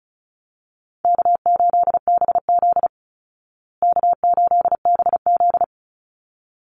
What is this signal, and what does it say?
Morse code, heard as a keyed tone, about 34 wpm, 700 Hz. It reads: K8BZ K8BZ
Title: Easy Call Signs Repeated 2x - 35wpm